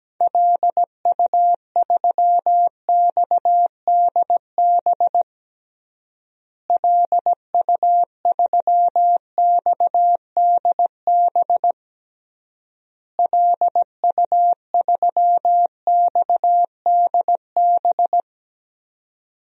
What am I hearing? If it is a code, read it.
Morse code, heard as a keyed tone, about 17 wpm, 700 Hz. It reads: LU3XDB LU3XDB LU3XDB
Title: DXCC-100 Call Signs - 6 Characters - Repeated 3x - 17wpm